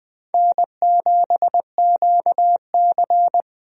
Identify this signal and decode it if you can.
Morse code, heard as a keyed tone, about 20 wpm, 700 Hz. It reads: N7QC
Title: USA Call Signs - 4 Characters - Repeated 2x - 20wpm